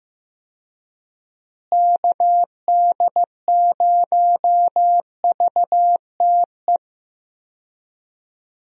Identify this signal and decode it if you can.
Morse code, heard as a keyed tone, about 16 wpm, 700 Hz. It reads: KD0VTE
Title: USA Call Signs - 6 Characters - Repeated 3x - 15wpm